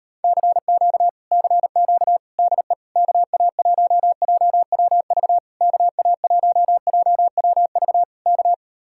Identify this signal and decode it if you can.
Morse code, heard as a keyed tone, about 38 wpm, 700 Hz. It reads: CQ CQ DE KA1JWV KA1JWV K